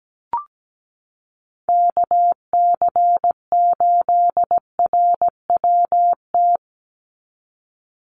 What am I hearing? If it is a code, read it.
Morse code, heard as a keyed tone, about 17 wpm, 700 Hz. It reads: KC8RWT
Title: USA Call Signs 17wpm